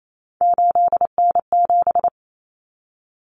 Morse code, heard as a keyed tone, about 28 wpm, 700 Hz. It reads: 8N7
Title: Common Cut Numbers - 3-Digit 28wpm